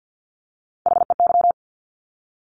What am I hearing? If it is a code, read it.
Morse code, heard as a keyed tone, about 50 wpm, 700 Hz. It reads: HEY
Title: Top Three Letter Combinations in Words 50wpm